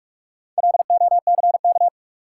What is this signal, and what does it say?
Morse code, heard as a keyed tone, about 45 wpm, 700 Hz. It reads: LOCK
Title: Sentences from Top 1300 Words - New Word Drill - 45wpm